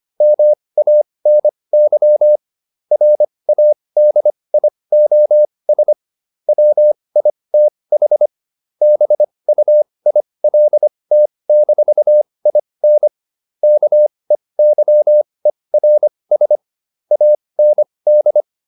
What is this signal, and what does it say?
Morse code, heard as a keyed tone, about 25 wpm, spaced as for 20 wpm, 600 Hz. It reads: MANY RADIOS WITH BUILT-IN KEYERS AND